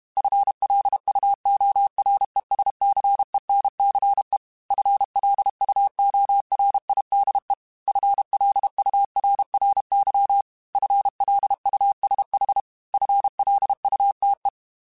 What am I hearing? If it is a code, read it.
Morse code, heard as a keyed tone, about 32 wpm, 800 Hz. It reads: FLUORESCENCE FLUORIDE FLURRY FLUSH FLUTE
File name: FL Runs / FL Runs mp3s / FL Runs @30